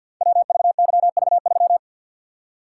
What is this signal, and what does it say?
Morse code, heard as a keyed tone, about 52 wpm, 700 Hz. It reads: WVYV3
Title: Code Groups - 5 Letter:Numbers - 50wpm